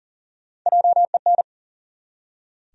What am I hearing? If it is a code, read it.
Morse code, heard as a keyed tone, about 38 wpm, 700 Hz. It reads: JEN
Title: USA Common Names - Max 4 Letters - 40wpm